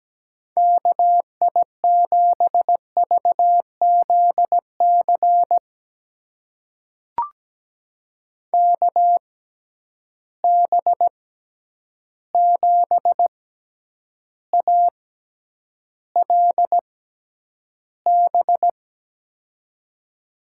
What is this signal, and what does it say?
Morse code, heard as a keyed tone, about 17 wpm, spaced as for 9 wpm, 700 Hz. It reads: KI7VZC KB7ALB
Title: USA Call Signs - 6 Characters - 6x Character Spacing - 17wpm